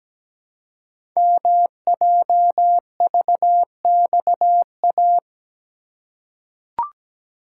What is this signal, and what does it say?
Morse code, heard as a keyed tone, about 17 wpm, 700 Hz. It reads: MJVXA
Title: Code Groups - 5 Letters - 17wpm